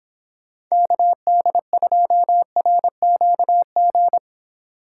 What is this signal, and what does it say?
Morse code, heard as a keyed tone, about 26 wpm, 700 Hz. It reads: KD2RQG